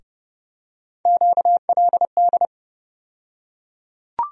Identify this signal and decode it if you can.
Morse code, heard as a keyed tone, about 31 wpm, 700 Hz. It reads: QLD
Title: USA Call Sign Postfixes - 3 Letters 30wpm